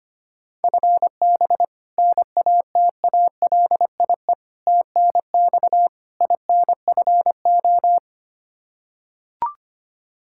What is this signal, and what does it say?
Morse code, heard as a keyed tone, about 25 wpm, 700 Hz. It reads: FB NATALIE TNX INFO
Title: Practice QSO - Exchange 3 - More Info - 25wpm